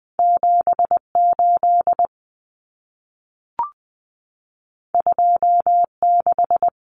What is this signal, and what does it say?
Morse code, heard as a keyed tone, about 20 wpm, 700 Hz. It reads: 78 26
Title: Common Cut Numbers - 2-Digit 20wpm